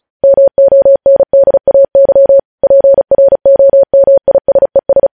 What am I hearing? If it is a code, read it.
Morse code, heard as a keyed tone, about 36 wpm, 550 Hz. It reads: MONDAY PROMISES
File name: podcast / News headlines-1x-35x35